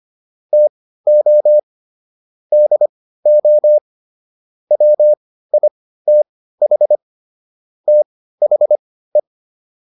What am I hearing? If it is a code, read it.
Morse code, heard as a keyed tone, about 25 wpm, spaced as for 15 wpm, 600 Hz. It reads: TO DO WITH THE
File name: WD305 15